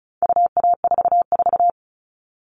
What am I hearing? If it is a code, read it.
Morse code, heard as a keyed tone, about 34 wpm, 700 Hz. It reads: UA44
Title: All Cut Numbers - 4-Digit 35wpm